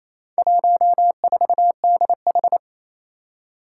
Morse code, heard as a keyed tone, about 27 wpm, 700 Hz. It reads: 14DH
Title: Four Challenging Letter-Number Combinations 28wpm